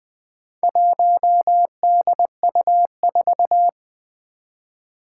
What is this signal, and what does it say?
Morse code, heard as a keyed tone, about 20 wpm, 700 Hz. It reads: 1DU4
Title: Four Challenging Letter-Number Combinations 20wpm